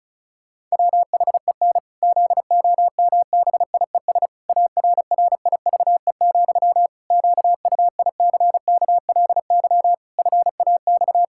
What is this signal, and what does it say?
Morse code, heard as a keyed tone, about 34 wpm, 700 Hz. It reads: WHEN ZOMBIES ARRIVE, QUICKLY FAX
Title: Pangram Challenge v2 - 35wpm